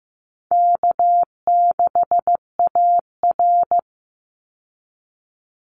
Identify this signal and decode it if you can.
Morse code, heard as a keyed tone, about 15 wpm, 700 Hz. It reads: K6AR
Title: Easy Call Signs Repeated 3x - 15wpm